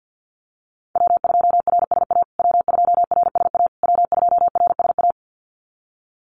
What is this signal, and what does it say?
Morse code, heard as a keyed tone, about 50 wpm, 700 Hz. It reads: W2RSA W2RSA W2RSA
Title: USA Call Signs Repeated 3x 50wpm